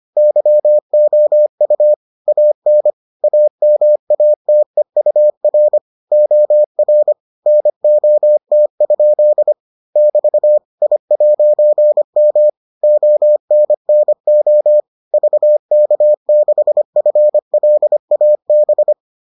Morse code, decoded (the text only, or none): YOU AN AMATEUR OR NOT? <BT> I'M ONNO VK6FLAB